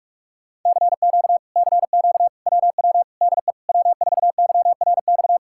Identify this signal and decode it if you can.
Morse code, heard as a keyed tone, about 49 wpm, 700 Hz. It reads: CQ CQ WW DE W4YRX